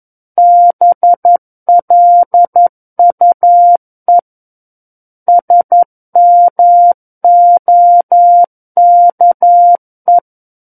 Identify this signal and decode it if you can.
Morse code, heard as a keyed tone, about 11 wpm, 700 Hz. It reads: BLUE SMOKE
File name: Book One / 02-Chapter 2 - 11 WPM